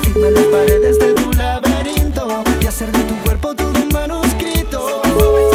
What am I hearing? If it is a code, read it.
Ringback tone (Europe, 425 Hz), with noise.